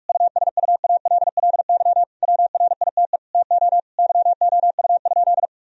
Morse code, heard as a keyed tone, about 41 wpm, 700 Hz. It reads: USUALLY WRITE TO YOU?
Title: Sentences from Top 1000 Words - Review Part 3 - 45wpm